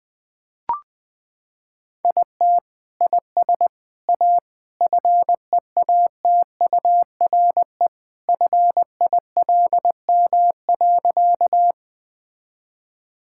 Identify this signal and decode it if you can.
Morse code, heard as a keyed tone, about 21 wpm, 700 Hz. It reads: IT IS A FEATURE FILM.
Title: Sentences from Top 1000 Words - Review Part 5 - 20wpm